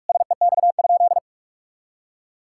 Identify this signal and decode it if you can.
Morse code, heard as a keyed tone, about 41 wpm, 700 Hz. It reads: SEX?
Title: Sentences from Top 1600 Words 45wpm